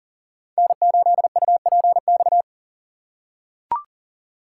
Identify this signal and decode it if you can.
Morse code, heard as a keyed tone, about 40 wpm, 690 Hz. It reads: N8UPX
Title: USA Call Signs - 5 Characters - 3x Character Spacing - 40wpm